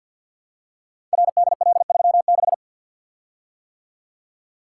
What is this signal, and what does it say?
Morse code, heard as a keyed tone, about 48 wpm, 700 Hz. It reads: RDL36
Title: Code Groups - 5 Letter:Numbers - 50wpm